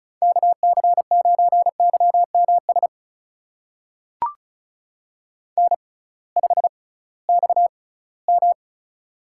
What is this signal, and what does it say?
Morse code, heard as a keyed tone, about 36 wpm, spaced as for 20 wpm, 700 Hz. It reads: KC9YMS N5XM
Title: USA Call Signs - 6x Character Spacing - 35wpm